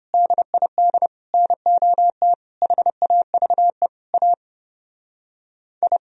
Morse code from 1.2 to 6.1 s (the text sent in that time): NOT HAVE A I